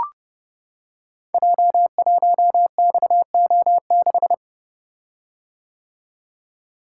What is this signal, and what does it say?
Morse code, heard as a keyed tone, about 30 wpm, 700 Hz. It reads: J1XO6
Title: Code Groups - 5 Letter:Numbers - 30wpm